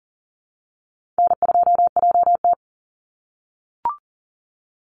Morse code, heard as a keyed tone, about 40 wpm, 700 Hz. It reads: N2JT
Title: Easy Call Signs - 40wpm